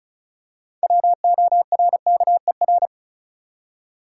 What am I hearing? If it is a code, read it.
Morse code, heard as a keyed tone, about 36 wpm, 700 Hz. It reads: WORKER
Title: Six Letter Words - 35wpm